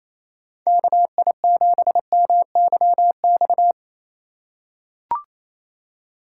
Morse code, heard as a keyed tone, about 27 wpm, 700 Hz. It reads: KI7MYX
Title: USA Call Signs - 6 Characters - 3x Character Spacing - 28wpm